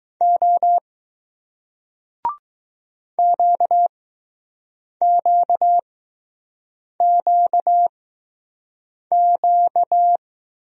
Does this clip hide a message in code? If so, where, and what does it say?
Yes: Morse code from 0.2 to 10.2 s, heard as a keyed tone, about 20 wpm, spaced as for 6 wpm, 700 Hz. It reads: O QQQQ